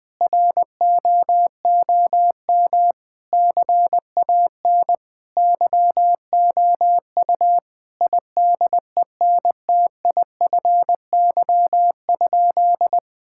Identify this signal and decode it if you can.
Morse code, heard as a keyed tone, about 21 wpm, 700 Hz. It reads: ROOM CAN YOU IDENTIFY?
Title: Sentences from Top 1000 Words - Review Part 6 - 20wpm